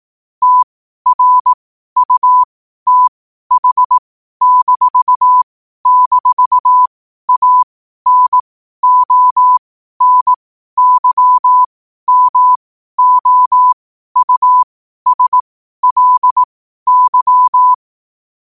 Morse code, heard as a keyed tone, about 18 wpm, spaced as for 12 wpm, 1000 Hz. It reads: TRUTH--ANONYMOUSLY
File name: qotdfiles / 12wpm